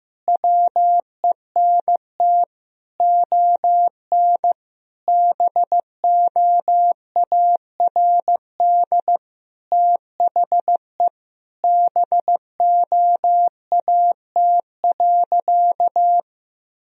Morse code, heard as a keyed tone, about 15 wpm, 700 Hz. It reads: WENT ON BOARD THE BOAT.
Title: Sentences from Top 1000 Words - Review Part 4 - 15wpm